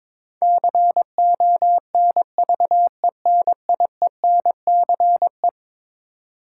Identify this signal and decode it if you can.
Morse code, heard as a keyed tone, about 22 wpm, 700 Hz. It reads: CONVENIENCE
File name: practice / 22wpm / Eleven Letter Words - 22wpm